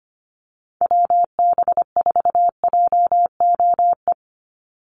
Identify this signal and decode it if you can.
Morse code, heard as a keyed tone, about 25 wpm, 690 Hz. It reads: WB4JOE
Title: USA Call Signs 25wpm